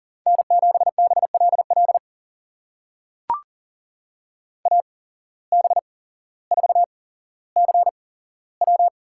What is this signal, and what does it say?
Morse code, heard as a keyed tone, about 40 wpm, spaced as for 17 wpm, 700 Hz. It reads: N7BLL AB4CW
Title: USA Call Signs - 5 Characters - 8x Character Spacing - 40wpm